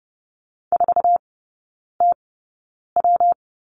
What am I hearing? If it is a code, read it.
Morse code, heard as a keyed tone, about 30 wpm, spaced as for 9 wpm, 700 Hz. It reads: 4TW
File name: Lesson 19e - Letter w - Three Characters 30wpm